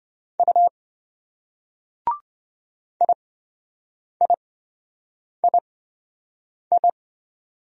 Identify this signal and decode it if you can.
Morse code, heard as a keyed tone, about 26 wpm, spaced as for 7 wpm, 700 Hz. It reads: U IIII